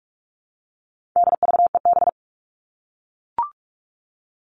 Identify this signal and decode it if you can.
Morse code, heard as a keyed tone, about 44 wpm, 700 Hz. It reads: DVEB
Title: All Cut Numbers - 4-Digit 45wpm